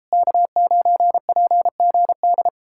Morse code, heard as a keyed tone, about 37 wpm, 700 Hz. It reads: K9PGD